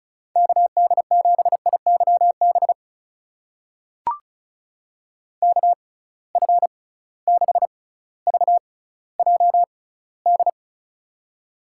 Morse code, heard as a keyed tone, about 36 wpm, spaced as for 18 wpm, 700 Hz. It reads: KD7IYB KF6VJD
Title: USA Call Signs - 6 Characters - 6x Character Spacing - 35wpm